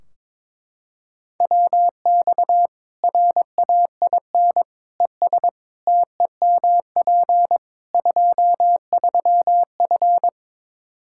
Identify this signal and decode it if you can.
Morse code, heard as a keyed tone, about 22 wpm, 700 Hz. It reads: WX RAIN ES TEMP 23F
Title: K1USN SST QSO Practice 22wpm